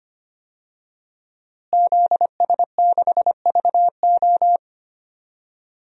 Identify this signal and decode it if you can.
Morse code, heard as a keyed tone, about 25 wpm, 700 Hz. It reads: ZS6VO
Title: DXCC-100 Call Signs - 5 Characters - Repeated 2x - 25wpm